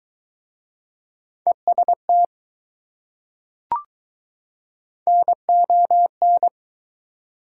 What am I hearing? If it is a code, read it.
Morse code, heard as a keyed tone, about 21 wpm, 700 Hz. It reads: EST NON